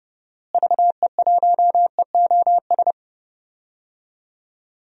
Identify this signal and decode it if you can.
Morse code, heard as a keyed tone, about 30 wpm, 700 Hz. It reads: VE1EOS
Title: DXCC-100 Call Signs - 6 Characters - Repeated 3x - 30wpm